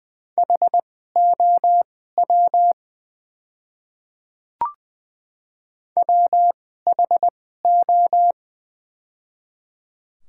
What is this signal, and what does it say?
Morse code, heard as a keyed tone, about 20 wpm, spaced as for 15 wpm, 690 Hz. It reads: HOW WHO
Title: 2x Character Spacing - Top 100 Words - 20wpm